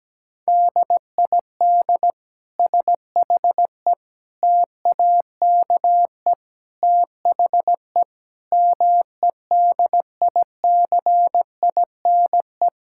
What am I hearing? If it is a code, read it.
Morse code, heard as a keyed tone, about 17 wpm, 700 Hz. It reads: DID SHE TAKE THE MEDICINE
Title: Sentences from Top 1000 Words - Review Part 6 - 17wpm